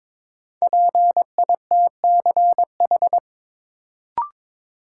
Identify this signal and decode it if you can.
Morse code, heard as a keyed tone, about 22 wpm, 700 Hz. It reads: PITCH
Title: Sentences from Top 1300 Words - New Word Drill - 22wpm